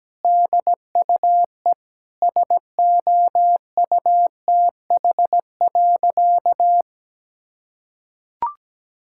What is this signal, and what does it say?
Morse code, heard as a keyed tone, about 17 wpm, 700 Hz. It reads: DUE SOUTH.
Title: Sentences from Top 1000 Words - Review Part 6 - 17wpm